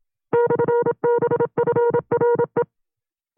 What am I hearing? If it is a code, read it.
Morse code, heard as a keyed tone, about 27 wpm, 480 Hz. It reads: /BFRE
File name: CW-practice/25WPM